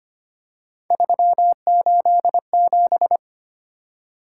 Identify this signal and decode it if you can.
Morse code, heard as a keyed tone, about 25 wpm, 700 Hz. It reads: 387
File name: Three Letter-Number Combinations - 25wpm